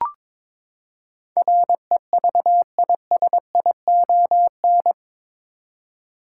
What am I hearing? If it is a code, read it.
Morse code, heard as a keyed tone, about 21 wpm, 700 Hz. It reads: REVISION
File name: practice / 22wpm / Eight Letter Words - 22wpm